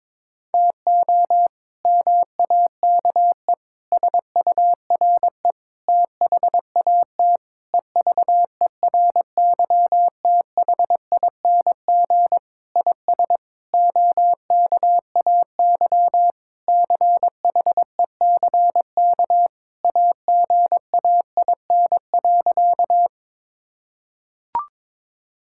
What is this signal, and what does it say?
Morse code, heard as a keyed tone, about 22 wpm, 700 Hz. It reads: TO MAKE SURE THAT EVERYTHING IS OKAY CHECK AGAIN.
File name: practice / 22wpm / Sentences with 4-Word Phrases - 22wpm